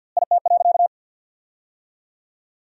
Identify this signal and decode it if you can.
Morse code, heard as a keyed tone, about 49 wpm, 700 Hz. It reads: IT.
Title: 2x Word-Spacing - Sentences from Top 500 Words - 50wpm